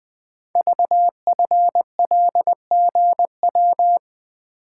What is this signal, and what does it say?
Morse code, heard as a keyed tone, about 20 wpm, 700 Hz. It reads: VFLGW
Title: Code Groups - 5 Letters - 20wpm